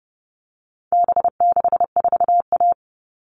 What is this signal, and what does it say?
Morse code, heard as a keyed tone, about 30 wpm, 700 Hz. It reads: B64A
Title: All Cut Numbers - 4-Digit 30wpm